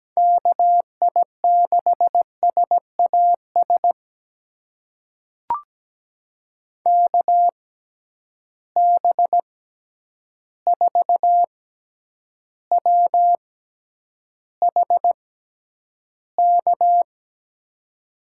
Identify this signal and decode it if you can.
Morse code, heard as a keyed tone, about 17 wpm, spaced as for 9 wpm, 700 Hz. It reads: KI6SAS KB4WHK
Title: USA Call Signs - 6 Characters - 6x Character Spacing - 17wpm